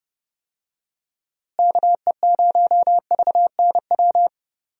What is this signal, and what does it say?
Morse code, heard as a keyed tone, about 30 wpm, 700 Hz. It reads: KE0VNW
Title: USA Call Signs - 6 Characters - Repeated 3x - 30wpm